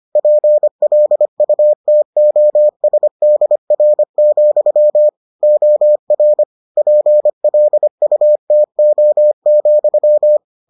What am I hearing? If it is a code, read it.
Morse code, heard as a keyed tone, about 25 wpm, 600 Hz. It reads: PLUTOSDR, OR PLUTO,